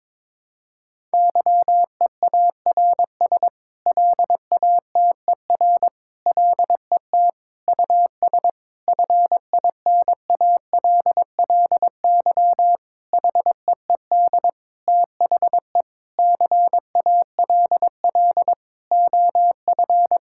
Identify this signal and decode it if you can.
Morse code, heard as a keyed tone, about 22 wpm, 700 Hz. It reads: YEARS LATER LET US FINALLY HEED THE CALL OF
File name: practice / 22wpm / Sentences with 8-Word Phrases - 22wpm